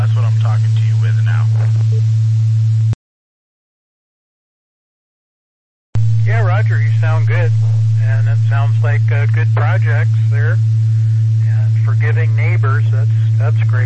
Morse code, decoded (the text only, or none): none